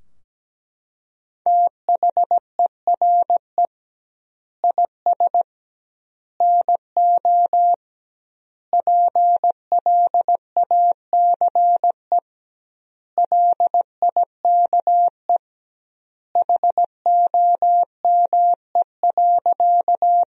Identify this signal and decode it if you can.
Morse code, heard as a keyed tone, about 17 wpm, 700 Hz. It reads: THERE IS NO PLACE LIKE HOME.
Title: 2x Word-Spacing - Sentences from Top 200 Words - 17wpm